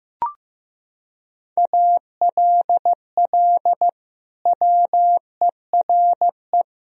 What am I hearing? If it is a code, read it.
Morse code, heard as a keyed tone, about 15 wpm, 700 Hz. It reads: ALL WERE
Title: Sentences from Top 1000 Words - Review Part 3 - 15wpm